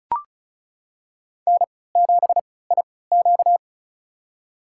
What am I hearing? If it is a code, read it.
Morse code, heard as a keyed tone, about 34 wpm, spaced as for 20 wpm, 700 Hz. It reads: N7IQ
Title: USA Call Signs - 4 Characters - 3x Character Spacing - 35wpm